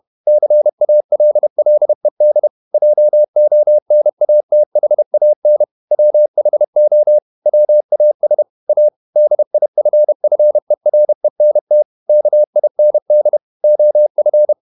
Morse code, heard as a keyed tone, about 31 wpm, 600 Hz. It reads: CALLED JONATHAN WHO WAS A DIFFERENT KIND OF
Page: Never 31wpm | CW med Gnister